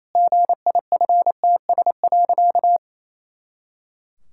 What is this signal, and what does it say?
Morse code, heard as a keyed tone, about 28 wpm, 700 Hz. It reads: GIFTS.
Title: Pangram Challenge v2 - 28wpm